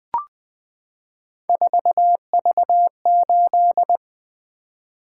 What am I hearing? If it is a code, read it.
Morse code, heard as a keyed tone, about 20 wpm, 700 Hz. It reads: 4V8
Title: Three Letter-Number Combinations - 20wpm